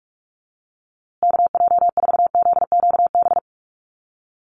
Morse code, heard as a keyed tone, about 44 wpm, 700 Hz. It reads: KJ4ZQB
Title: USA Call Signs Repeated 2x 45wpm